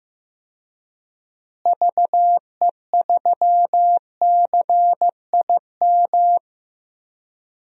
Morse code, heard as a keyed tone, about 15 wpm, 700 Hz. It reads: VE3CIM
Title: DXCC-100 Call Signs - 6 Characters - Repeated 2x - 15wpm